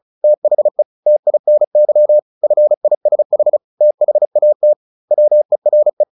Morse code, heard as a keyed tone, about 34 wpm, 600 Hz. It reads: THE TINY FISH THAT WERE
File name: the old man and the sea / the old man and the sea 1 35wpm